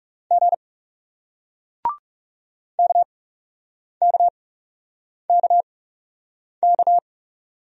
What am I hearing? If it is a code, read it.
Morse code, heard as a keyed tone, about 40 wpm, spaced as for 8 wpm, 700 Hz. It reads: G KKKK